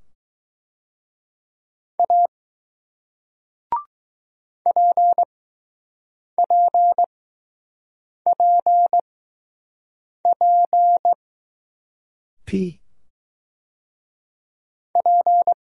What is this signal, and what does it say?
Morse code, heard as a keyed tone, about 20 wpm, spaced as for 6 wpm, 700 Hz. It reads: A PPPP P